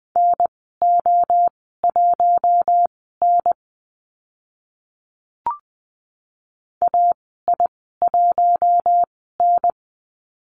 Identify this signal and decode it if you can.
Morse code, heard as a keyed tone, about 20 wpm, spaced as for 14 wpm, 700 Hz. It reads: NO1N AI1N
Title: 042 Lesson 8g - Number 1 - Call Signs 20wpm